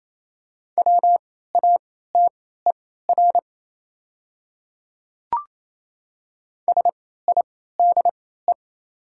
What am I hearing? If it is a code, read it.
Morse code, heard as a keyed tone, about 28 wpm, spaced as for 16 wpm, 700 Hz. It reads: WATER SIDE
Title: 3x Character Spacing - Top 100 Words - 28wpm